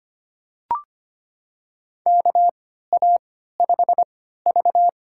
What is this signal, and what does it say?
Morse code, heard as a keyed tone, about 25 wpm, spaced as for 14 wpm, 700 Hz. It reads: KA5V
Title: USA Call Signs - 3x Character Spacing - 25wpm